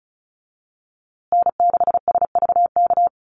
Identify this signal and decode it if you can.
Morse code, heard as a keyed tone, about 34 wpm, 700 Hz. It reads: N6SVK
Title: USA Call Signs Repeated 2x 35wpm